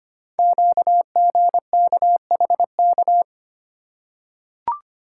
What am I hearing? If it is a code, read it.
Morse code, heard as a keyed tone, about 25 wpm, 700 Hz. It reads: QGKHK
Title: Code Groups - 5 Letters - 25wpm